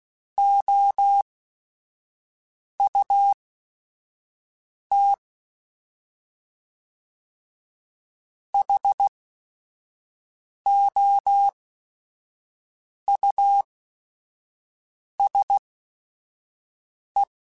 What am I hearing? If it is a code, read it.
Morse code, heard as a keyed tone, about 16 wpm, spaced as for 4 wpm, 800 Hz. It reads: OUTHOUSE